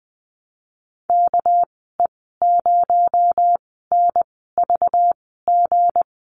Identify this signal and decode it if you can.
Morse code, heard as a keyed tone, about 20 wpm, spaced as for 15 wpm, 700 Hz. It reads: KE0NVG
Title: 245 Lesson 39g - Letter z - Call Signs 20wpm